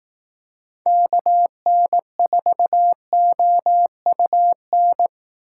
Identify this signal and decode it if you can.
Morse code, heard as a keyed tone, about 18 wpm, 700 Hz. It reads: KN4OUN
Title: Speed Racing - USA 2x3 Call Signs 15wpm